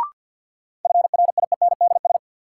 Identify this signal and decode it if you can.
Morse code, heard as a keyed tone, about 51 wpm, 700 Hz. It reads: FRIENDS
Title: Sentences from Top 1000 Words - Review Part 5 - New Word Drill - 50wpm